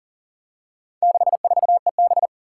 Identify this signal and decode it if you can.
Morse code, heard as a keyed tone, about 40 wpm, 700 Hz. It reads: 64EB
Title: Four Challenging Letter-Number Combinations 40wpm